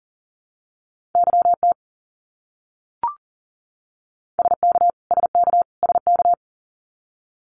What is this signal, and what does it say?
Morse code, heard as a keyed tone, about 40 wpm, 700 Hz. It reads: YT SK SK SK
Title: Canadian Provinces and Territories - Repeated 3x - 40wpm